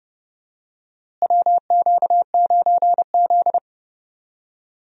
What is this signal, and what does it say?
Morse code, heard as a keyed tone, about 30 wpm, 700 Hz. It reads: WQ9Z
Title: USA Call Signs - 4 Characters - Repeated 3x - 30wpm